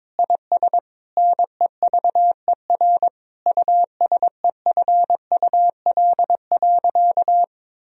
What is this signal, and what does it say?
Morse code, heard as a keyed tone, about 22 wpm, 700 Hz. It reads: IS NEVER USEFUL.
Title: Sentences from Top 1000 Words - Review Part 4 - 22wpm